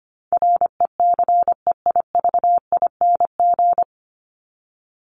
Morse code, heard as a keyed tone, about 25 wpm, 700 Hz. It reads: RECEIVING
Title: ING Suffix - Top 100 Words with Base Words 25wpm